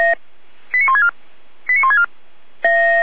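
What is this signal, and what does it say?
Five-tone selective call (ZVEI), heard as quick runs of five tones, tones of about 70 ms. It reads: 89164 89164
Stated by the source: ZVEI five-tone call 89250 received by SDR on channel 455, recorded by feuerwehr-orlamuende.org fire brigade server.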